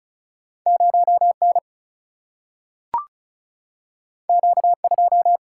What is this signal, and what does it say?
Morse code, heard as a keyed tone, about 36 wpm, 700 Hz. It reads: 0 N Q 2
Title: Single-Letter Single-Number 35wpm